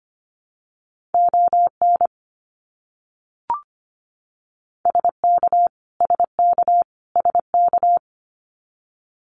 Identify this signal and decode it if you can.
Morse code, heard as a keyed tone, about 25 wpm, 700 Hz. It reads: ON SK SK SK
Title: Canadian Provinces and Territories - Repeated 3x - 25wpm